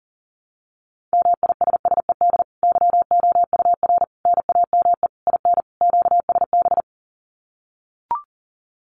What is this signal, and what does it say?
Morse code, heard as a keyed tone, about 40 wpm, 700 Hz. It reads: MISSED YOUR NAME IN QSB
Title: Common QSO Phrases 40wpm